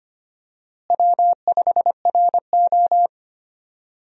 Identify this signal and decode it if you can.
Morse code, heard as a keyed tone, about 25 wpm, 700 Hz. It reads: W5RO
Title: USA Call Signs - 4 Characters - Repeated 2x - 25wpm